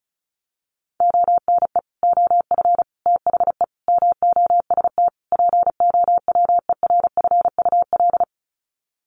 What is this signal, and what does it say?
Morse code, heard as a keyed tone, about 34 wpm, 700 Hz. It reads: ONE OF THE MOST POWERFUL
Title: Sets of 5 Words Encore 35wpm